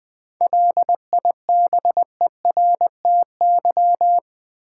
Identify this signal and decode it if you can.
Morse code, heard as a keyed tone, about 21 wpm, 700 Hz. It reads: LIBERTY
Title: 7 Letter Words - First 3 Letters Given - 20wpm